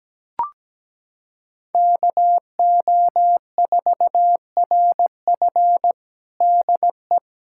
Morse code, heard as a keyed tone, about 17 wpm, 700 Hz. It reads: KO4RF DE
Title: USA Callsign DE Callsign - 5 Characters - 17wpm